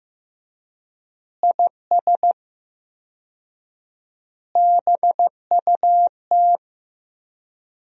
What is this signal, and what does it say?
Morse code, heard as a keyed tone, about 15 wpm, 700 Hz. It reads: IS BUT
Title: 4x Word-Spacing - Sentences from Top 100 Words - 15wpm